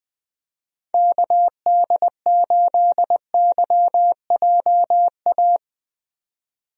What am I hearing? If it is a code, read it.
Morse code, heard as a keyed tone, about 20 wpm, 700 Hz. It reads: KD8YJA
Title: USA Call Signs - 6 Characters - 20wpm